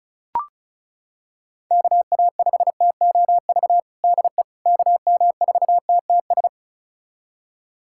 Morse code, heard as a keyed tone, about 34 wpm, 700 Hz. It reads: KA5TOV DE KM4TTS
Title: USA Callsign DE Callsign - 6 Characters - 35wpm